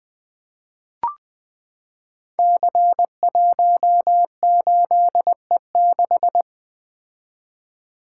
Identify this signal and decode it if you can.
Morse code, heard as a keyed tone, about 20 wpm, 700 Hz. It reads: C18E6
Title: Code Groups - 5 Letter:Numbers - 20wpm